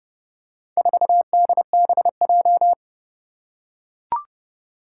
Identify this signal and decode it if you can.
Morse code, heard as a keyed tone, about 32 wpm, 700 Hz. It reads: 4DBJ
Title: Four Challenging Letter-Number Combinations 30wpm